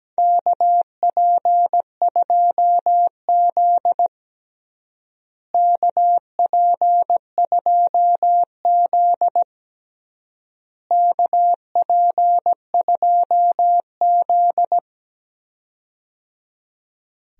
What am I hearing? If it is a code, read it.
Morse code, heard as a keyed tone, about 17 wpm, 700 Hz. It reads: KP2Z KP2Z KP2Z
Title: DXCC-100 Call Signs - 4 Characters - Repeated 3x - 17wpm